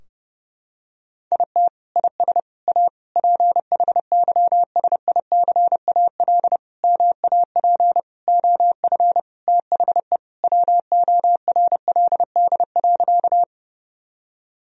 Morse code, heard as a keyed tone, about 30 wpm, 700 Hz. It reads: IT IS A PHYSICAL MAP OF THE WORLD.
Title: Sentences from Top 1000 Words - Review Part 4 - 30wpm